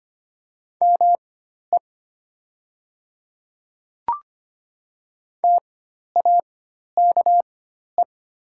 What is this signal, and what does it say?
Morse code, heard as a keyed tone, about 25 wpm, spaced as for 12 wpm, 700 Hz. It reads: ME TAKE
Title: 4x Character Spacing - Top 500 Words - 25wpm